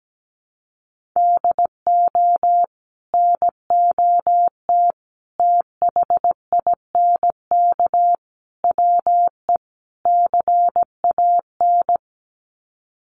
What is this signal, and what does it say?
Morse code, heard as a keyed tone, about 17 wpm, 700 Hz. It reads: DO NOT THINK WE CAN
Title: Sets of 5 Words Encore2 17wpm